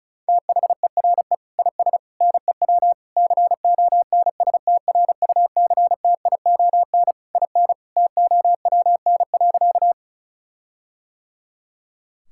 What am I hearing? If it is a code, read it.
Morse code, heard as a keyed tone, about 36 wpm, 700 Hz. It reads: THERE IS NEW CONSTRUCTION IN TOWN.
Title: Sentences from Top 1600 Words 35wpm